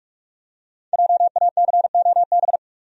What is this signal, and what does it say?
Morse code, heard as a keyed tone, about 46 wpm, 700 Hz. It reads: JACOB
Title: USA Common Names - Max 5 Letters - 45wpm